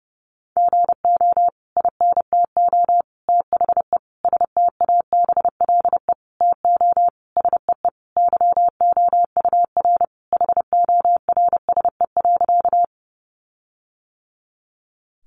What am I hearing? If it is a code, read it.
Morse code, heard as a keyed tone, about 30 wpm, 700 Hz. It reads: GO INTO THE STABLE TO SEE YOUR HORSE.
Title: Sentences from Top 1500 Words 30wpm